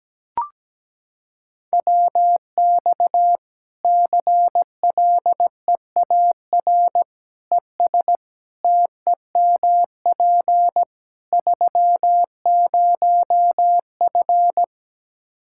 Morse code, heard as a keyed tone, about 17 wpm, 700 Hz. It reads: WX CLEAR ES TEMP 30F
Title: Practice QSO - Exchange 2 - Info - 17wpm